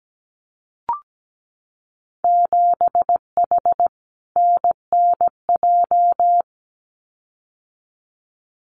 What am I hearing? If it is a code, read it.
Morse code, heard as a keyed tone, about 17 wpm, 700 Hz. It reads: 7H NNJ
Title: Winter Field Day Exchanges Encore 17wpm